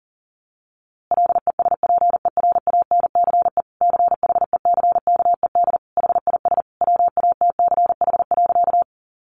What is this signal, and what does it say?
Morse code, heard as a keyed tone, about 37 wpm, 700 Hz. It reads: LESPERANCE CHECKED HIS WATCH.
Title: A Sound Of Thunder 40wpm